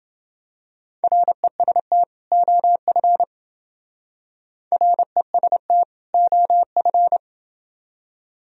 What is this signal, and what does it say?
Morse code, heard as a keyed tone, about 28 wpm, 700 Hz. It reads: REST OF REST OF